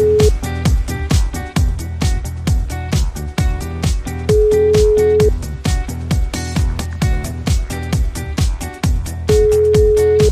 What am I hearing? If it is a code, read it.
Ringback tone (Europe, 425 Hz).